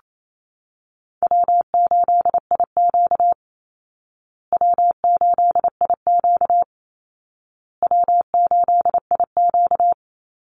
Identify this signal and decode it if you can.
Morse code, heard as a keyed tone, about 28 wpm, 700 Hz. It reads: W8IQ W8IQ W8IQ
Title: Easy Call Signs Repeated 3x - 28wpm